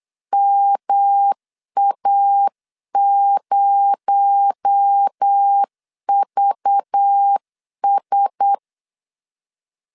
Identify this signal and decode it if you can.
Morse code, heard as a keyed tone, about 8 wpm, 800 Hz. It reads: MA0VS
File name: LEZIONI / 7 GF9Z / 7 G 8wpm 3118